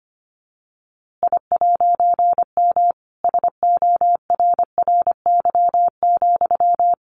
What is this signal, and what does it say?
Morse code, heard as a keyed tone, about 25 wpm, 700 Hz. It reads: I'M SORRY,
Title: A Sound Of Thunder 25wpm